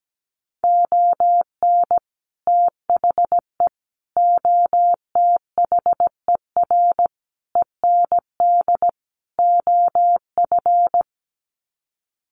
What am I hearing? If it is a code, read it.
Morse code, heard as a keyed tone, about 17 wpm, 700 Hz. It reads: ON THE OTHER END OF
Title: Sets of 5 Words Encore 17wpm